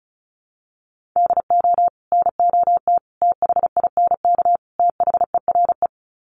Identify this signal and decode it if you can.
Morse code, heard as a keyed tone, about 34 wpm, 700 Hz. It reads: DO NOT THINK THERE
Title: Sets of 4 Words 35wpm